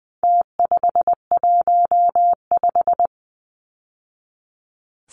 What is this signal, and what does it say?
Morse code, heard as a keyed tone, about 20 wpm, 700 Hz. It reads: T515
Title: Numbers - Contest Style - 4-Digit 20wpm